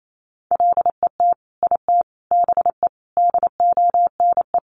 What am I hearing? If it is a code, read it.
Morse code, heard as a keyed tone, about 28 wpm, 700 Hz. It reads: LET IT BE DONE
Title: Sentences from Top 1100 Words 28wpm